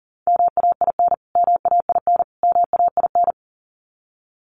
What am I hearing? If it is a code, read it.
Morse code, heard as a keyed tone, about 40 wpm, 700 Hz. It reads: MAIN MAIN MAIN
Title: Top 500 Words Repeated 40wpm